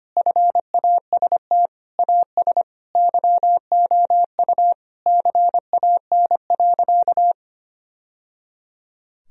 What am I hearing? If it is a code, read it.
Morse code, heard as a keyed tone, about 25 wpm, 700 Hz. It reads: FAST AS YOU CAN.
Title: Sentences from Top 1000 Words - Review Part 2 - 25wpm